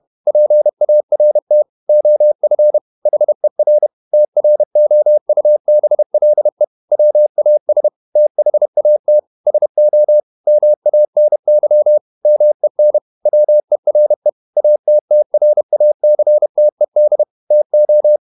Morse code, heard as a keyed tone, about 31 wpm, 600 Hz. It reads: PART OF HER TROUBLE WAS THAT SO MANY MEN WERE ATTRACTED TO
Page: Never 31wpm | CW med Gnister